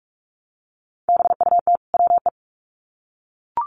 Morse code, heard as a keyed tone, about 44 wpm, 700 Hz. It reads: BUT WE
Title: Sets of 2 Words 45wpm